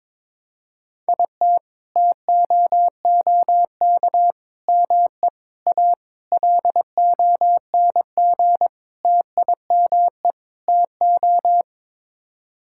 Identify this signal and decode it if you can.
Morse code, heard as a keyed tone, about 22 wpm, 700 Hz. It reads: IT TOOK ME A LONG TIME TO
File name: practice / 22wpm / Sets of 7 Words - 22wpm